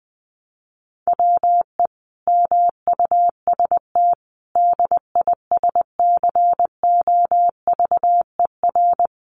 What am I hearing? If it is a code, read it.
Morse code, heard as a keyed tone, about 20 wpm, 700 Hz. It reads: WE MUST DISCOVER
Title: Sentences from Top 1100 Words 20wpm